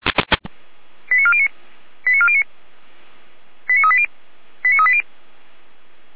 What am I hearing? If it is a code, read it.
Five-tone selective call (ZVEI), heard as quick runs of five tones, tones of about 70 ms. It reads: 89409 89409 89380 89380
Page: /455/SDR/ZVEI/